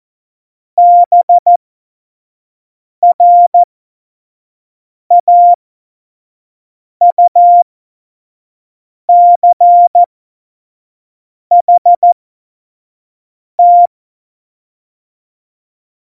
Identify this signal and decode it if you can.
Morse code, heard as a keyed tone, about 14 wpm, spaced as for 5 wpm, 700 Hz. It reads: BRAUCHT
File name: 25 F70